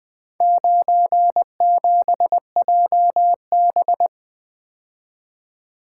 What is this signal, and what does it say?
Morse code, heard as a keyed tone, about 20 wpm, 700 Hz. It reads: 97JB
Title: Four Letter-Number Combinations - 20wpm